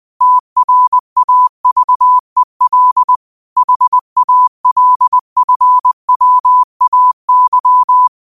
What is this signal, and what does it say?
Morse code, heard as a keyed tone, about 20 wpm, 1000 Hz. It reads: TRAVEL HALFWAY
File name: qotdfiles / 20wpm